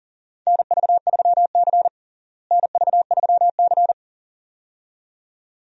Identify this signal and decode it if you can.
Morse code, heard as a keyed tone, about 40 wpm, 700 Hz. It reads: NV3C NV3C
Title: USA Call Signs - 4 Characters - Repeated 2x - 40wpm